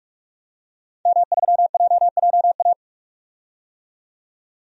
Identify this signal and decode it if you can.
Morse code, heard as a keyed tone, about 42 wpm, 700 Hz. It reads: M3JJA
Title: DXCC-100 Call Signs - 5 Characters - 45wpm